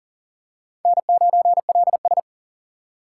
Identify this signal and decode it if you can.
Morse code, heard as a keyed tone, about 40 wpm, 700 Hz. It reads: N9LS
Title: USA Call Signs - 2x Character Spacing - 40wpm